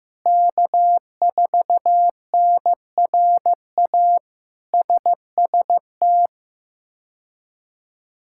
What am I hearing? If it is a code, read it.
Morse code, heard as a keyed tone, about 15 wpm, 700 Hz. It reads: K4NRA SST
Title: K1USN SST QSO Practice - 15wpm